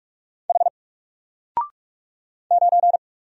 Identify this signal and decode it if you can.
Morse code, heard as a keyed tone, about 46 wpm, 700 Hz. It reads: H 9
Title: Single Letter-Number Alt 45wpm